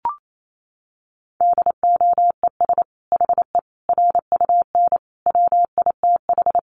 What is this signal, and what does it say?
Morse code, heard as a keyed tone, about 28 wpm, 700 Hz. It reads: DOES HE RUN WITH
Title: Sentences from Top 1300 Words 28wpm